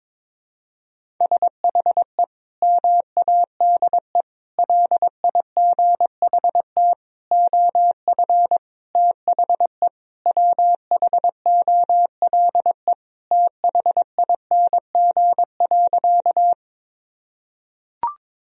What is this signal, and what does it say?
Morse code, heard as a keyed tone, about 22 wpm, 700 Hz. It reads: SHE MADE LIGHT OF THE WHOLE THING.
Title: Sentences from Top 1000 Words - Review Part 1 - 22wpm